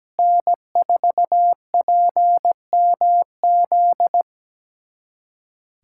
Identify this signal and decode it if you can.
Morse code, heard as a keyed tone, about 17 wpm, 700 Hz. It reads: N4PMZ
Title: USA Callsign DE Callsign - 5 Characters - 17wpm